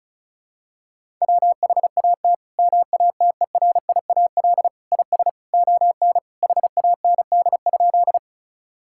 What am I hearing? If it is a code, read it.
Morse code, heard as a keyed tone, about 36 wpm, 700 Hz. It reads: WHAT MATERIAL IS ON HAND?
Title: Sentences from Top 1000 Words - Review Part 5 - 35wpm